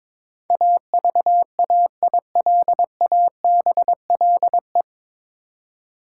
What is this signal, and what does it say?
Morse code, heard as a keyed tone, about 22 wpm, 700 Hz. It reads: AVAILABLE
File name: 9 Letter Words - First 3 Letters Given - 22wpm